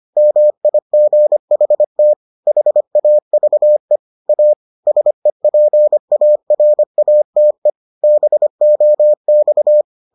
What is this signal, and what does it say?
Morse code, heard as a keyed tone, about 25 wpm, 600 Hz. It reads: MIGHT HAVE A SEPARATE BOX